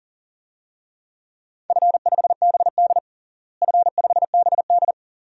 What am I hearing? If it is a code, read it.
Morse code, heard as a keyed tone, about 40 wpm, 700 Hz. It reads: F5BD F5BD
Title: DXCC-100 Call Signs - 4 Characters - Repeated 2x - 40wpm